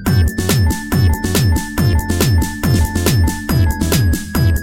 There is no code.